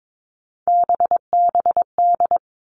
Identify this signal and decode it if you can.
Morse code, heard as a keyed tone, about 22 wpm, 700 Hz. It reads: BBD
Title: All Cut Numbers - 3-Digit 22wpm